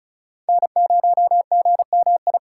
Morse code, heard as a keyed tone, about 36 wpm, 700 Hz. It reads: N0GMI
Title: USA Call Signs - 5 Characters - 2x Character Spacing - 35wpm